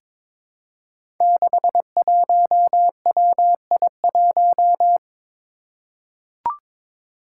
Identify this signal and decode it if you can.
Morse code, heard as a keyed tone, about 22 wpm, 700 Hz. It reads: 61WI1
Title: Five Challenging Letter-Number Combinations 22wpm